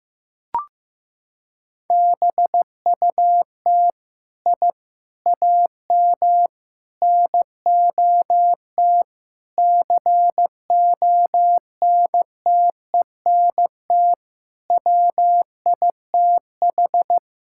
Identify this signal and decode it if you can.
Morse code, heard as a keyed tone, about 15 wpm, 700 Hz. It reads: BUT I AM NOT CONTENT WITH
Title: Sentences from Top 1000 Words - Review Part 6 - 15wpm